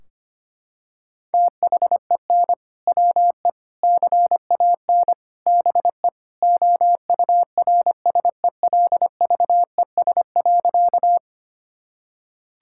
Morse code, heard as a keyed tone, about 25 wpm, 700 Hz. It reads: THEN WE CAN BE OURSELVES.
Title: Sentences from Top 1600 Words 25wpm